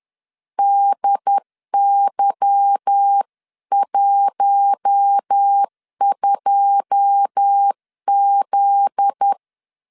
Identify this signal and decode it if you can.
Morse code, heard as a keyed tone, about 10 wpm, 800 Hz. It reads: DY12Z
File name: LEZIONI / 9 28 10wpm 2842